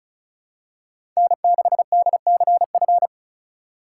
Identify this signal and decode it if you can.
Morse code, heard as a keyed tone, about 36 wpm, 700 Hz. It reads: N6DCF
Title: USA Call Signs - 5 Characters - 3x Character Spacing - 35wpm